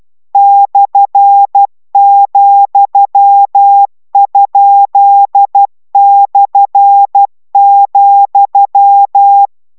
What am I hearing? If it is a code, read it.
Morse code, heard as a keyed tone, about 12 wpm, 800 Hz. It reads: /,?/,